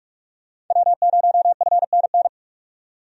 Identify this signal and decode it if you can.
Morse code, heard as a keyed tone, about 46 wpm, 700 Hz. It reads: W0FNN
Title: USA Call Signs - 5 Characters - 45wpm